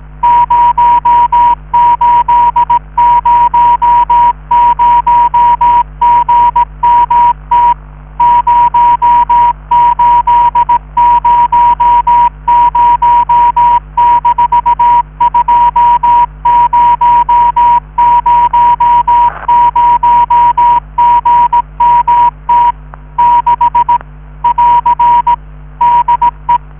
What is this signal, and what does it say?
Morse code, heard as a keyed tone, about 18 wpm, 960 Hz. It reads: 0800GMT 0800-2000GMT 6 <AR> DE